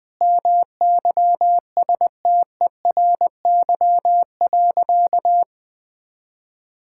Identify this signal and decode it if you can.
Morse code, heard as a keyed tone, about 20 wpm, 700 Hz. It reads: MYSTERY.
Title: Sentences with 7-Word Phrases - 20wpm